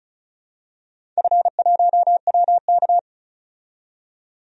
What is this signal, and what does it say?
Morse code, heard as a keyed tone, about 36 wpm, 700 Hz. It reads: F1WK
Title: DXCC-100 Call Signs - 4 Characters - 35wpm